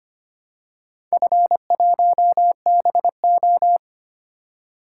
Morse code, heard as a keyed tone, about 25 wpm, 700 Hz. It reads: F1BO
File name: practice / 25wpm / Four Letter-Number Combinations - 25wpm